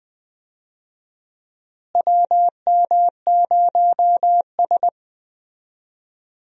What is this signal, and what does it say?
Morse code, heard as a keyed tone, about 20 wpm, 700 Hz. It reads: WM0S
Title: USA Call Signs - 4 Characters - Repeated 3x - 20wpm